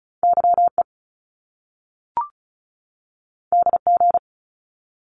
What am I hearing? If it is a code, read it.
Morse code, heard as a keyed tone, about 36 wpm, 700 Hz. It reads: YE DG
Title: Top Two Letter Combinations in Words 35wpm